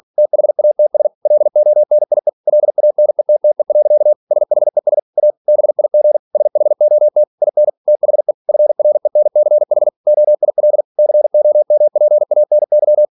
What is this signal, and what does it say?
Morse code, heard as a keyed tone, about 46 wpm, 600 Hz. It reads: THATS LONIE LANETTE. SHES A BIG SHOT IN THE FRENCH OIL COMPANY